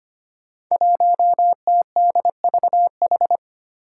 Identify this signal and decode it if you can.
Morse code, heard as a keyed tone, about 25 wpm, 700 Hz. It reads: 1TDVH
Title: Five Challenging Letter-Number Combinations 25wpm